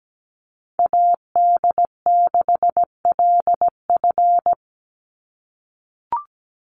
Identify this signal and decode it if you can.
Morse code, heard as a keyed tone, about 17 wpm, 700 Hz. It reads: AD6LF
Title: USA Call Signs 17wpm